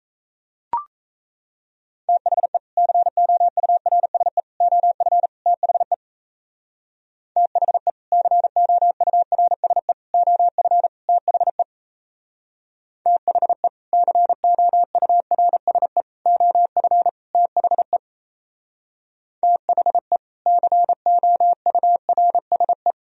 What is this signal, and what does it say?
Morse code, heard as a keyed tone, about 34 wpm, 700 Hz. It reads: THE COURSE OF THE THE COURSE OF THE THE COURSE OF THE THE COURSE
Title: Speed-Racing - Sets of 4 Words 28wpm